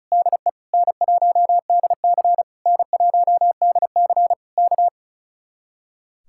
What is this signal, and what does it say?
Morse code, heard as a keyed tone, about 34 wpm, 700 Hz. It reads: DE N1DC N1DC K
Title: Speed-Racing Common QSO Phrases - 35wpm